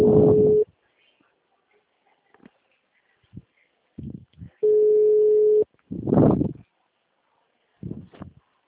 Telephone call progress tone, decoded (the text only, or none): ringback tone